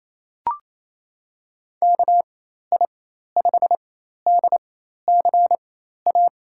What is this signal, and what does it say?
Morse code, heard as a keyed tone, about 27 wpm, spaced as for 13 wpm, 700 Hz. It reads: KI5DCA
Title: USA Call Signs - 4x Character Spacing - 28wpm